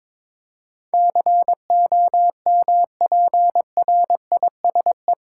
Morse code, heard as a keyed tone, about 22 wpm, 700 Hz. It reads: COMPRISE
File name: practice / 22wpm / Eight Letter Words - 22wpm